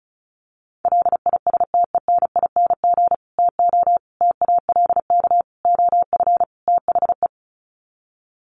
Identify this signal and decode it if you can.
Morse code, heard as a keyed tone, about 34 wpm, 700 Hz. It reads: LISTENING TO TALK OF THE
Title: Sets of 5 Words 35wpm